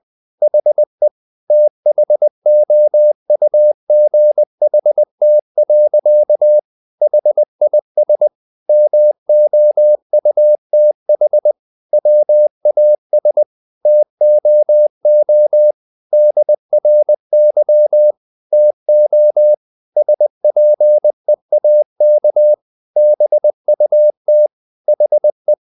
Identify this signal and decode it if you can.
Morse code, heard as a keyed tone, about 20 wpm, 600 Hz. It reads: HE THOUGHT. HIS MOUTH WAS TOO DRY TO SPEAK BUT HE
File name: the old man and the sea part 2 20wpm